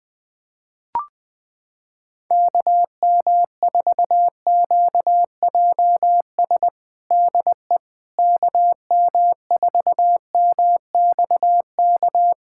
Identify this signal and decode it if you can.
Morse code, heard as a keyed tone, about 20 wpm, 700 Hz. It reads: KM4QJS DE KM4MXK
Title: USA Callsign DE Callsign - 6 Characters - 20wpm